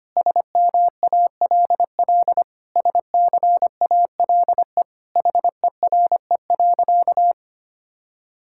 Morse code, heard as a keyed tone, about 25 wpm, 700 Hz. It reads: SMALL SCALE HERE.
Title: Sentences from Top 1000 Words - Review Part 5 - 25wpm